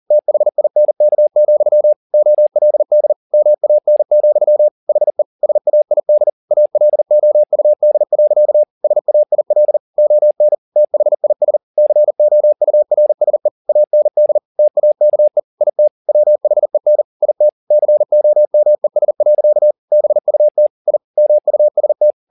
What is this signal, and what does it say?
Morse code, heard as a keyed tone, about 40 wpm, 600 Hz. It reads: THINK, OLD MAN, HE SAID ALOUD. SAIL ON THIS COURSE AND TAKE IT WHEN IT COMES. BUT I MUST
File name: the old man and the sea 2 40wpm